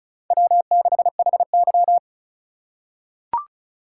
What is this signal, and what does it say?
Morse code, heard as a keyed tone, about 34 wpm, 700 Hz. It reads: W6HY
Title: USA Call Signs - 4 Characters - 4x Character Spacing - 35wpm